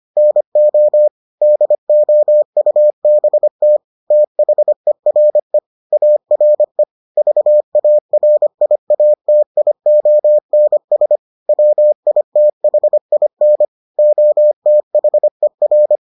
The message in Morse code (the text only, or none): NO DOUBT THERE ARE VARIATIONS WITHIN OTHER